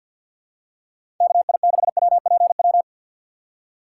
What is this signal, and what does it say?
Morse code, heard as a keyed tone, about 52 wpm, 700 Hz. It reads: KI6WPW
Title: USA Call Signs - 2x Character Spacing - 50wpm